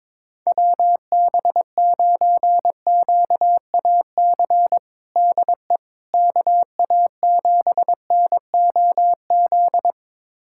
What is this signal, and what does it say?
Morse code, heard as a keyed tone, about 22 wpm, 700 Hz. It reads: WB9QAC DE KA7NOZ
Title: USA Callsign DE Callsign - 6 Characters - 22wpm